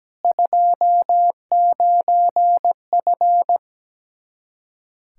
Morse code, heard as a keyed tone, about 17 wpm, 700 Hz. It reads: 29F
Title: Practice QSO - Exchange 2 - Info - 17wpm